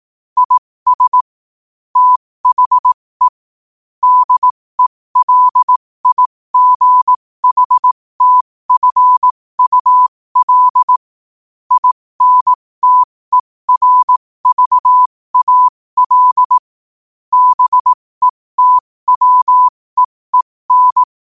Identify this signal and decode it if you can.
Morse code, heard as a keyed tone, about 18 wpm, spaced as for 15 wpm, 1000 Hz. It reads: IS THE DELIGHTFUL INTERVAL BETWEEN